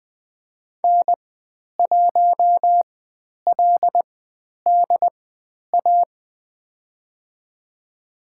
Morse code, heard as a keyed tone, about 20 wpm, spaced as for 10 wpm, 700 Hz. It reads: N1LDA
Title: 109 Lesson 18g - Letter m - Call Signs 20wpm